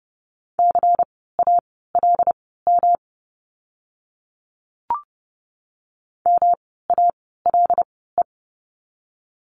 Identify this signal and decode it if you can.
Morse code, heard as a keyed tone, about 30 wpm, spaced as for 17 wpm, 700 Hz. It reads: CALM MALE
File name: course / 30:10wpm / 108 Lesson 18f - Letter m - Words 30wpm